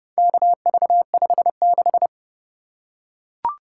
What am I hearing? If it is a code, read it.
Morse code, heard as a keyed tone, about 30 wpm, 700 Hz. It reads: KV56